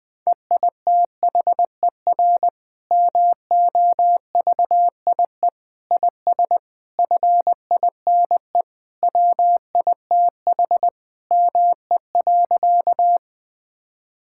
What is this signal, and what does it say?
Morse code, heard as a keyed tone, about 21 wpm, 700 Hz. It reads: EITHER MOVIE IS FINE WITH ME.
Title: Sentences from Top 1000 Words - Review Part 5 - 20wpm